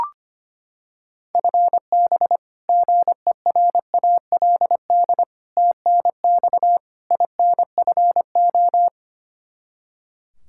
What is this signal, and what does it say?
Morse code, heard as a keyed tone, about 25 wpm, 700 Hz. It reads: FB GERALD TNX INFO
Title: Practice QSO - Exchange 3 - More Info - 25wpm